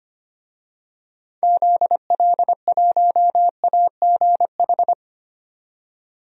Morse code, heard as a keyed tone, about 25 wpm, 700 Hz. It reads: ZL1AGH
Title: DXCC-100 Call Signs - 6 Characters - 25wpm